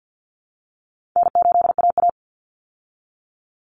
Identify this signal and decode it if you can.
Morse code, heard as a keyed tone, about 50 wpm, 700 Hz. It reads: N8AA
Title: Easy Call Signs Repeated 2x - 50wpm